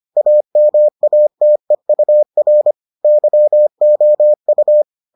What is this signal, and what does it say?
Morse code, heard as a keyed tone, about 25 wpm, 600 Hz. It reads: AMATEUR YOU